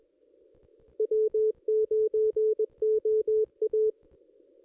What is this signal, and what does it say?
Morse code, heard as a keyed tone, about 21 wpm, 430 Hz. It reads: W9OA
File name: contests / ss-cw-k5tr-op-n5rz -1800